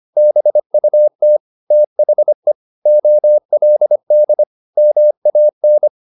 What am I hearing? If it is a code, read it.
Morse code, heard as a keyed tone, about 25 wpm, 600 Hz. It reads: BUT THE OLD MAN
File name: the old man and the sea part 1 25wpm